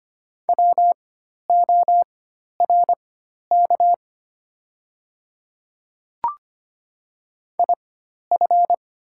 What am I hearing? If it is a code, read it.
Morse code, heard as a keyed tone, about 25 wpm, spaced as for 12 wpm, 700 Hz. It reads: WORK IF
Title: 4x Character Spacing - Top 200 Words - 25wpm